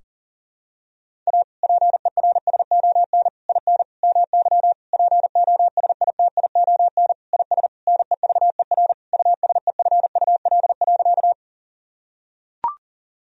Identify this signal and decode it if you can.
Morse code, heard as a keyed tone, about 40 wpm, 700 Hz. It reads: A PERSON IN MY POSITION IS NEVER USEFUL.
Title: Sentences from Top 1000 Words - Review Part 4 - 40wpm